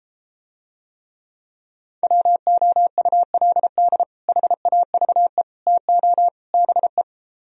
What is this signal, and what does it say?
Morse code, heard as a keyed tone, about 33 wpm, 700 Hz. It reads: WOULD HAVE TO BE